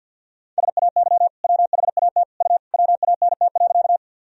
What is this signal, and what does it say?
Morse code, heard as a keyed tone, about 51 wpm, 700 Hz. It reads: SAY WHAT U WANT.
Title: Sentences from Top 200 Words and Abbreviations 50wpm